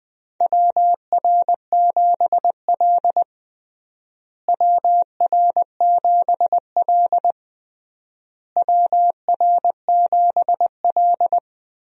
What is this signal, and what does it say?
Morse code, heard as a keyed tone, about 20 wpm, 700 Hz. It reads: WR7L WR7L WR7L
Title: USA Call Signs - 4 Characters - Repeated 3x - 20wpm